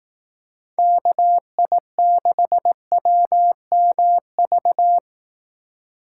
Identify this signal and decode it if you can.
Morse code, heard as a keyed tone, about 18 wpm, 700 Hz. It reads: KI6WMV
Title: Speed Racing - USA 2x3 Call Signs 15wpm